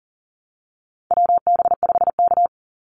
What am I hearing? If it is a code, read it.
Morse code, heard as a keyed tone, about 40 wpm, 700 Hz. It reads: WB5K
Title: Call Signs Repeated 2x - 40wpm